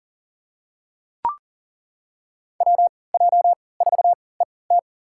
Morse code, heard as a keyed tone, about 40 wpm, spaced as for 23 wpm, 700 Hz. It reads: WJ4ET
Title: USA Call Signs - 5 Characters - 3x Character Spacing - 40wpm